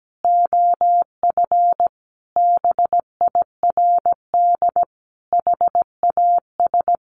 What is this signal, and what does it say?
Morse code, heard as a keyed tone, about 17 wpm, 700 Hz. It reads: OF BIRD HAS
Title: Sentences from Top 1200 Words 17wpm